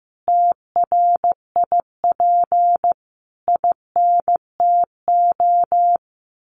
Morse code, heard as a keyed tone, about 15 wpm, 700 Hz. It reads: TRIP INTO
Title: Sentences from Top 1300 Words 15wpm